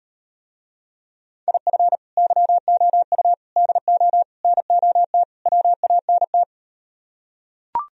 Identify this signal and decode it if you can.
Morse code, heard as a keyed tone, about 38 wpm, 700 Hz. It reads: IF YOU DO NOT WANT